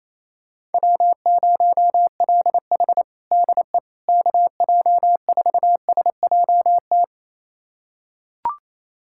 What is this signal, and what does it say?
Morse code, heard as a keyed tone, about 28 wpm, 700 Hz. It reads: W0LH DE KJ4SJT
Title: USA Callsign DE Callsign 28wpm